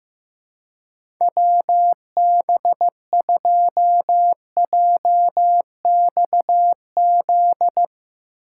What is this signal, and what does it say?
Morse code, heard as a keyed tone, about 15 wpm, 700 Hz. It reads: WB2JXZ
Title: USA Call Signs - 6 Characters - 3x Character Spacing - 15wpm